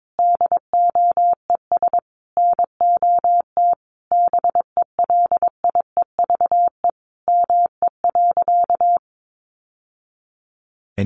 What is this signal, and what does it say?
Morse code, heard as a keyed tone, about 22 wpm, 700 Hz. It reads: DOES NOT BELIEVE ME.
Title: Sentences from Top 1000 Words 22wpm